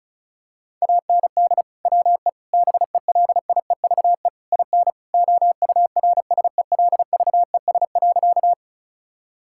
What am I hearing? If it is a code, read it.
Morse code, heard as a keyed tone, about 34 wpm, 700 Hz. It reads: AND WE BELIEVE IN OURSELVES.
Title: Sentences from Top 1600 Words 35wpm